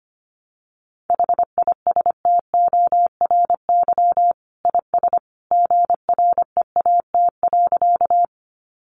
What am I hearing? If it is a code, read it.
Morse code, heard as a keyed tone, about 25 wpm, 700 Hz. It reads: HISTORY IS GREAT.
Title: Sentences from Top 600 Words 25wpm